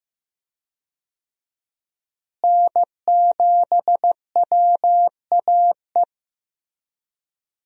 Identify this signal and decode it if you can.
Morse code, heard as a keyed tone, about 15 wpm, 700 Hz. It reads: N7WAE
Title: USA Call Signs - 5 Characters - Repeated 2x - 15wpm